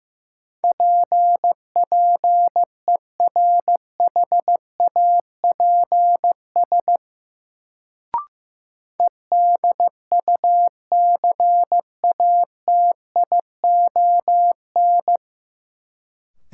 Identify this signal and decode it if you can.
Morse code, heard as a keyed tone, about 15 wpm, 700 Hz. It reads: P PERHAPS EDUCATION
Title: Sentences from Top 1100 Words - New Word Drill Alt - 15wpm